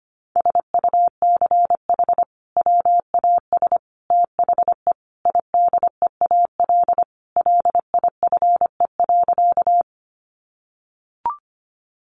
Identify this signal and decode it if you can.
Morse code, heard as a keyed tone, about 25 wpm, 700 Hz. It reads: SUCH WAS THE IDEAL LIFE.
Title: Sentences from Top 1400 Words 25wpm